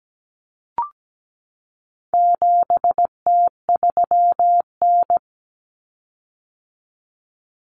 Morse code, heard as a keyed tone, about 17 wpm, 700 Hz. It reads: 7T3N
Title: Common Cut Numbers - 4-Digit 17wpm